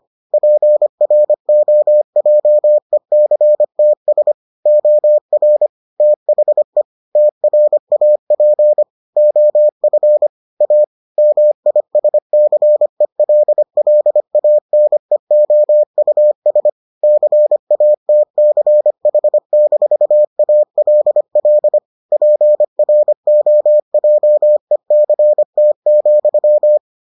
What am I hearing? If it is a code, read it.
Morse code, heard as a keyed tone, about 25 wpm, 600 Hz. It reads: PROJECTS OR THE TRAP OF A MISCELLANEOUS CATCH-ALL PROJECT,